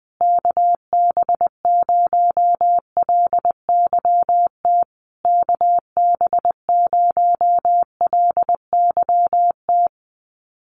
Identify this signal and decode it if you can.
Morse code, heard as a keyed tone, about 20 wpm, 700 Hz. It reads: KB0LYT KB0LYT
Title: USA Call Signs Repeated 3x 20wpm